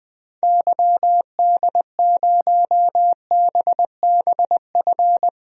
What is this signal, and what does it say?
Morse code, heard as a keyed tone, about 20 wpm, 700 Hz. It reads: YD0BBF
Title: DXCC-100 Call Signs - 6 Characters - 20wpm